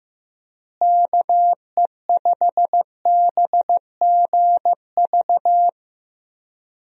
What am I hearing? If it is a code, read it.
Morse code, heard as a keyed tone, about 15 wpm, 700 Hz. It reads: KE5BGV
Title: USA Call Signs - 6 Characters - Repeated 3x - 15wpm